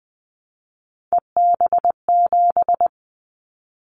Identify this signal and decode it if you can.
Morse code, heard as a keyed tone, about 20 wpm, 700 Hz. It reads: EB7
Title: Callsign Prefixes for Top 25 DXCC - 20wpm